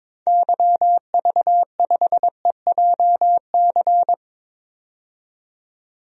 Morse code, heard as a keyed tone, about 21 wpm, 700 Hz. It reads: YV5EJC
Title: DXCC-100 Call Signs - 6 Characters - 22wpm